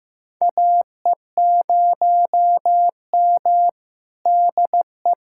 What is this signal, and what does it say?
Morse code, heard as a keyed tone, about 15 wpm, 700 Hz. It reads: AE0M DE
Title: USA Callsign DE Callsign - 4 Characters - 15wpm